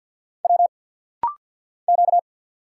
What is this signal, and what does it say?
Morse code, heard as a keyed tone, about 50 wpm, 700 Hz. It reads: W Q
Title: Single Letters - Rapid-Fire 50wpm